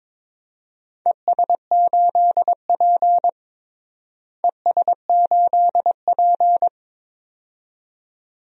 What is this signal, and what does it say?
Morse code, heard as a keyed tone, about 21 wpm, 700 Hz. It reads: ES8P ES8P
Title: DXCC-100 Call Signs - 4 Characters - Repeated 2x - 22wpm